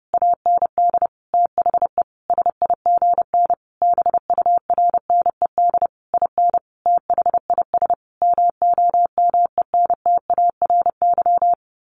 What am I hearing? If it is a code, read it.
Morse code, heard as a keyed tone, about 28 wpm, 700 Hz. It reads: AND THE SIGN BURNED IN THIS MOMENTARY
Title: A Sound Of Thunder 30wpm